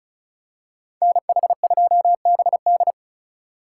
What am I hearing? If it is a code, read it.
Morse code, heard as a keyed tone, about 36 wpm, 700 Hz. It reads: NH2BD
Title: USA Call Signs - 5 Characters - 3x Character Spacing - 35wpm